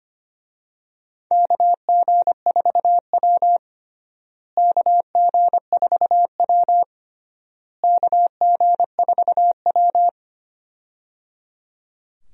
Morse code, heard as a keyed tone, about 25 wpm, 700 Hz. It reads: KG4W KG4W KG4W
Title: USA Call Signs - 4 Characters - Repeated 3x - 25wpm